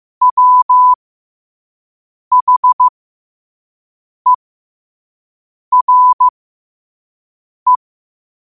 Morse code, heard as a keyed tone, about 15 wpm, spaced as for 5 wpm, 1000 Hz. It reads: WHERE
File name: qotdfiles / 5wpm